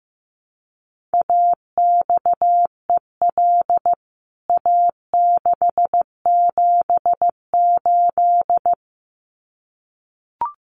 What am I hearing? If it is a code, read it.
Morse code, heard as a keyed tone, about 15 wpm, 700 Hz. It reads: AXEL A678
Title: CWT Exchange 15wpm